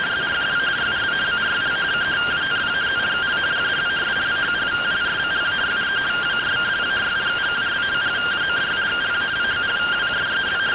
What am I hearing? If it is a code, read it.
Two-tone teleprinter signal, about 170 Hz shift.
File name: findings / CT AXIAL -256.gen-01 rtty rx